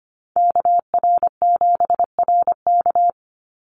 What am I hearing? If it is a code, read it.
Morse code, heard as a keyed tone, about 25 wpm, 700 Hz. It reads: KR7RK
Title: NA SOTA Chaser Callsigns 25wpm